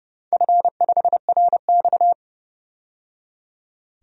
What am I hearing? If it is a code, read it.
Morse code, heard as a keyed tone, about 31 wpm, 700 Hz. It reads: F5RX
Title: DXCC-100 Call Signs - 4 Characters - 30wpm